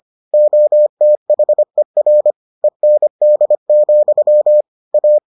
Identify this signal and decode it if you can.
Morse code, heard as a keyed tone, about 25 wpm, 600 Hz. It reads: OTHER END, A